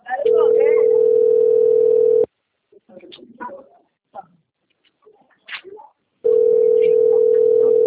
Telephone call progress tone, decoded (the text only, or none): ringback tone